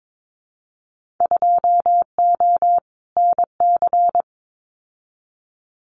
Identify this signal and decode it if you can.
Morse code, heard as a keyed tone, about 22 wpm, 700 Hz. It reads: 2O NC
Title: Winter Field Day Exchanges 22wpm